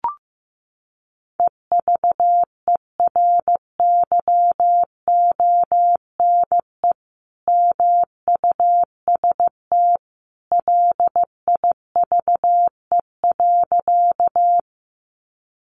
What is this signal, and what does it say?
Morse code, heard as a keyed tone, about 15 wpm, 700 Hz. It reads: EVERYONE MUST LIVE.
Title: Sentences from Top 800 Words 15wpm